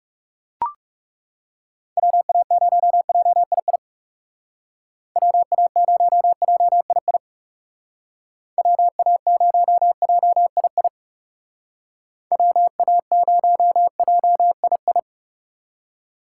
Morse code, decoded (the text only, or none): WA0JII WA0JII WA0JII WA0JII